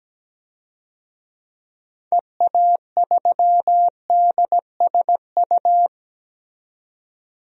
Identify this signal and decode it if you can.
Morse code, heard as a keyed tone, about 17 wpm, 700 Hz. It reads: EA3DSU
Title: DXCC-100 Call Signs - 6 Characters - 17wpm